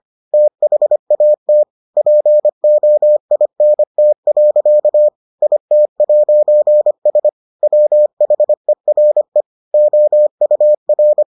Morse code, heard as a keyed tone, about 25 wpm, 600 Hz. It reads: THAT POINT. IT'S WHERE OUR